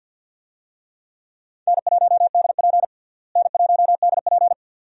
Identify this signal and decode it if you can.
Morse code, heard as a keyed tone, about 51 wpm, 700 Hz. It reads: N1DP N1DP
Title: USA Call Signs - 4 Characters - Repeated 2x - 50wpm